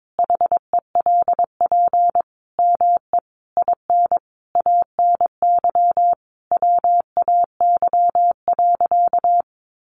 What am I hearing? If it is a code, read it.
Morse code, heard as a keyed tone, about 22 wpm, 700 Hz. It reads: HELP ME IN ANY WAY.
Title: Sentences from Top 600 Words 22wpm